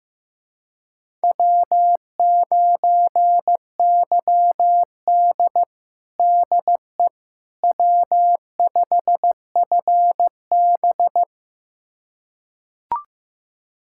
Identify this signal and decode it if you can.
Morse code, heard as a keyed tone, about 15 wpm, 700 Hz. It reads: W9YD DE W5FB
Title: USA Callsign DE Callsign - 4 Characters - 15wpm